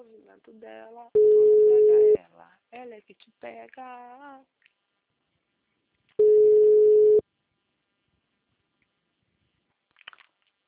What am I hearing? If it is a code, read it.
Ringback tone (Europe, 425 Hz), with speech.